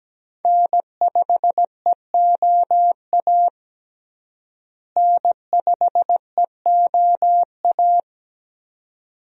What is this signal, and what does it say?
Morse code, heard as a keyed tone, about 17 wpm, 700 Hz. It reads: N5EOA N5EOA
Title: USA Call Signs - 5 Characters - Repeated 2x - 17wpm